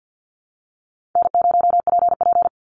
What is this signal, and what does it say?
Morse code, heard as a keyed tone, about 50 wpm, 700 Hz. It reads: N0PP
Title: Call Signs Repeated 2x - 50wpm